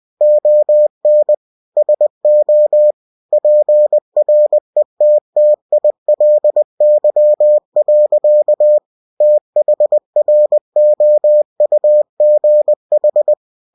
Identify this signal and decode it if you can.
Morse code, heard as a keyed tone, about 20 wpm, 600 Hz. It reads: ON SO PRETTILY. THROUGH
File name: cw-books / winnie-the-pooh / winnie-20wpm-page-0028